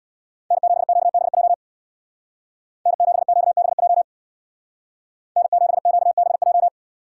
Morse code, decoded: N7QBJ N7QBJ N7QBJ

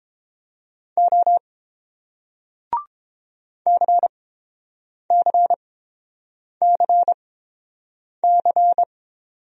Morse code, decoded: O CCCC

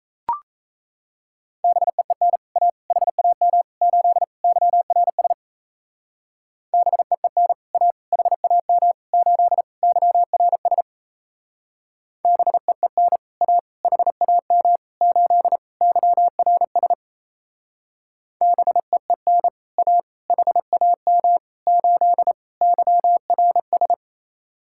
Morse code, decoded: BEEN A HAM 8 YRS BEEN A HAM 8 YRS BEEN A HAM 8 YRS BEEN A HAM 8 YRS

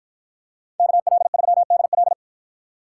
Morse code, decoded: KL3DL

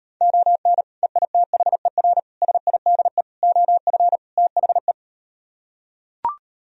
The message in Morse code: ON EITHER SIDE OF THE

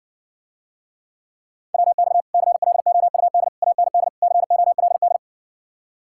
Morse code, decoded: WX CLOUD AND COLD

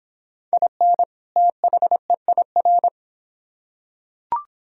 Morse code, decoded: IN THEIR

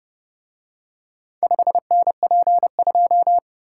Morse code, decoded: 5NP2

